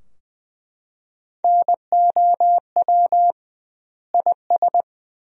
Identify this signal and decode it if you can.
Morse code, heard as a keyed tone, about 20 wpm, 700 Hz. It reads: NOW IS
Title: 2x Word-Spacing - Sentences from Top 100 Words - 20wpm